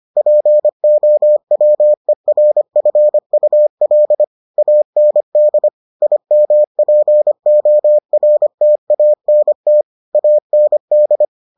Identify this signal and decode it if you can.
Morse code, heard as a keyed tone, about 25 wpm, 600 Hz. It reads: POWERFUL AND IMPORTANT AND